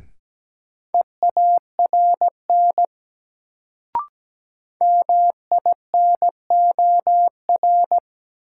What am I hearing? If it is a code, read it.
Morse code, heard as a keyed tone, about 17 wpm, 700 Hz. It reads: EARN MINOR
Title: Sentences from Top 1500 Words - New Word Drill - 17wpm